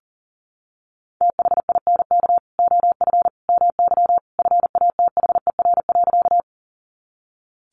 Morse code, heard as a keyed tone, about 40 wpm, 700 Hz. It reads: THINK OF MY FATHER.
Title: Sentences from Top 300 Words 40wpm